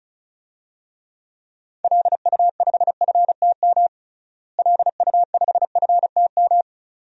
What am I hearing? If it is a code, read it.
Morse code, heard as a keyed tone, about 36 wpm, 700 Hz. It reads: LU5FTM LU5FTM
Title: DXCC-100 Call Signs - 6 Characters - Repeated 2x - 35wpm